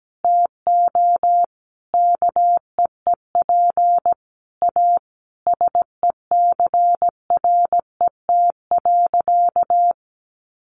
Morse code, heard as a keyed tone, about 17 wpm, 700 Hz. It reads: TO KEEP A SECRET.
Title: Sentences from Top 1300 Words 17wpm